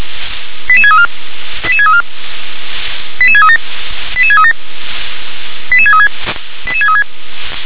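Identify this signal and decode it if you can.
Five-tone selective call (ZVEI), heard as quick runs of five tones, tones of about 70 ms. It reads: 88634 88634 88637 88637 88636 88636
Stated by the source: ZVEI five-tone selective call received on an SDR.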